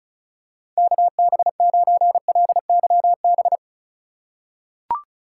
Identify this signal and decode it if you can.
Morse code, heard as a keyed tone, about 34 wpm, 700 Hz. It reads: KB9LYB
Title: USA Call Signs - 6 Characters - 4x Character Spacing - 35wpm